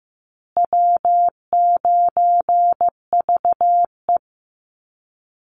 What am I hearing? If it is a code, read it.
Morse code, heard as a keyed tone, about 15 wpm, 700 Hz. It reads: W9VE
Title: Call Signs Repeated 2x - 15wpm